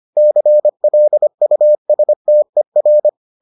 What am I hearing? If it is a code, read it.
Morse code, heard as a keyed tone, about 25 wpm, 600 Hz. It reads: CLUSTER